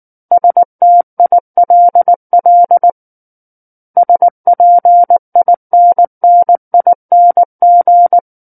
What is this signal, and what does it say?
Morse code, heard as a keyed tone, about 19 wpm, 700 Hz. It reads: STILL SPINNING
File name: Book One / 10-Chapter 10 - 19 WPM